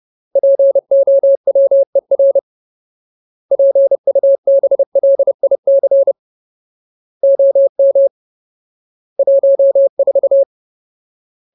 Morse code, heard as a keyed tone, about 30 wpm, 550 Hz. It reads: POWER PUBLIC OM 14
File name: Ultimate Word List / 30WPM Ultimate